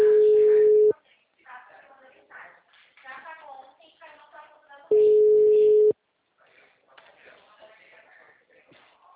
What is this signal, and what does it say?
Ringback tone (Europe, 425 Hz), with speech.
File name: out- -2012- - - .0